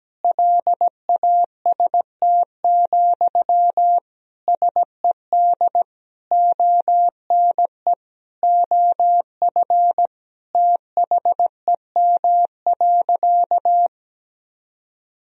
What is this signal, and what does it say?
Morse code, heard as a keyed tone, about 17 wpm, 700 Hz. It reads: LAST, SED ONE OF THEM.
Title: Sentences from Top 300 Words and Abbreviations 17wpm